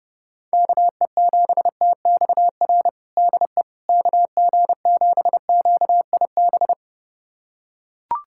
Morse code, heard as a keyed tone, about 32 wpm, 700 Hz. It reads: KE7TXR DE KG7QIB
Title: USA Callsign DE Callsign - 6 Characters - 30wpm